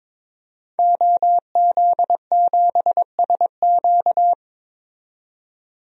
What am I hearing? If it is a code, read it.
Morse code, heard as a keyed tone, about 22 wpm, 700 Hz. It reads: OZ7SQ
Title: DXCC-100 Call Signs - 5 Characters - 22wpm